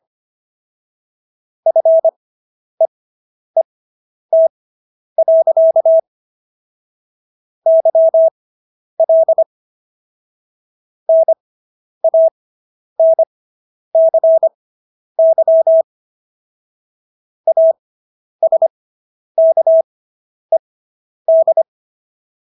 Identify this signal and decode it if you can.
Morse code, heard as a keyed tone, about 25 wpm, spaced as for 10 wpm, 650 Hz. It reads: FEET. YL NANCY ASKED